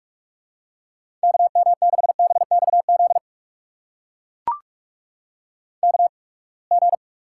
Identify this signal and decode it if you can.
Morse code, heard as a keyed tone, about 44 wpm, spaced as for 30 wpm, 700 Hz. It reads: KM6BXZ KG
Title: USA Call Signs - 6 Characters - 8x Character Spacing - 45wpm